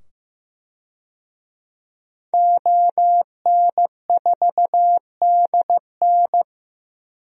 Morse code, heard as a keyed tone, about 15 wpm, 700 Hz. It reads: ON4DN
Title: DXCC-100 Call Signs - 5 Characters - 15wpm